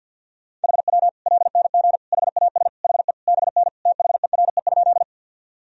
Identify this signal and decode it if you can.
Morse code, heard as a keyed tone, about 53 wpm, 700 Hz. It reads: HW LNG HAS HE BN THERE?
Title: Sentences from Top 100 Words and Abbreviations 50wpm